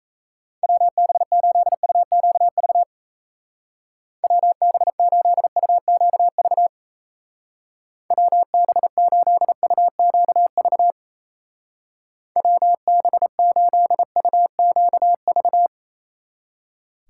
Morse code, heard as a keyed tone, about 34 wpm, 700 Hz. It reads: WB8UQV WB8UQV WB8UQV WB8UQV
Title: Speed Racing - USA 2x3 Call Signs 28wpm